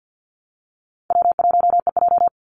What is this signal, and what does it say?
Morse code, heard as a keyed tone, about 50 wpm, 700 Hz. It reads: W1EJ
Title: NA SOTA Activator Callsigns 50wpm